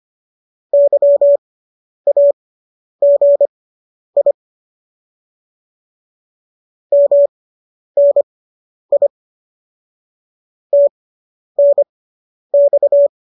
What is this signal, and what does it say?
Morse code, heard as a keyed tone, about 25 wpm, spaced as for 10 wpm, 570 Hz. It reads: YAGI MNI TNX
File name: qso219 10